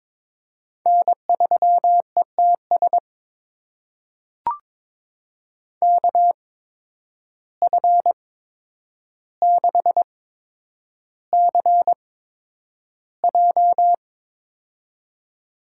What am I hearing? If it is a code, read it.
Morse code, heard as a keyed tone, about 21 wpm, spaced as for 9 wpm, 700 Hz. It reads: N3ETS KF6CJ
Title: USA Call Signs - 5 Characters - 8x Character Spacing - 22wpm